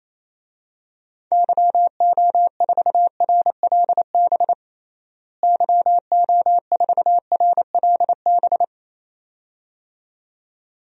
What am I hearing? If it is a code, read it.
Morse code, heard as a keyed tone, about 28 wpm, 700 Hz. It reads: YO4RLB YO4RLB
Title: DXCC-100 Call Signs - 6 Characters - Repeated 2x - 28wpm